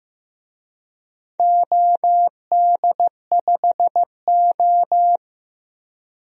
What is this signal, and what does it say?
Morse code, heard as a keyed tone, about 15 wpm, 700 Hz. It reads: OD5O
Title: DXCC-100 Call Signs - 4 Characters - 15wpm